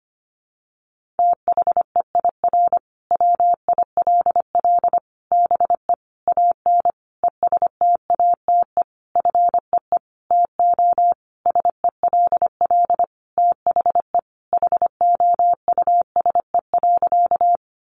Morse code, heard as a keyed tone, about 25 wpm, 700 Hz. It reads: THEIR WILL BE AN ESTATE FEE TO SELL THE HOUSE.
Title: Sentences from Top 1500 Words 25wpm